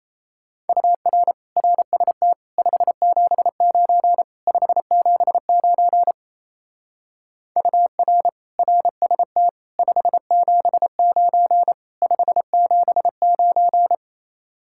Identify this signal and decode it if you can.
Morse code, heard as a keyed tone, about 33 wpm, 700 Hz. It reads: UR RST 579 579 UR RST 579 579
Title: Speed-Racing Common QSO Phrases Encore - 28wpm